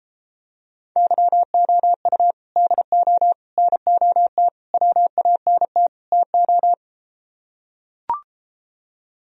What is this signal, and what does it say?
Morse code, heard as a keyed tone, about 33 wpm, 700 Hz. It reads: YOU DO NOT WANT TO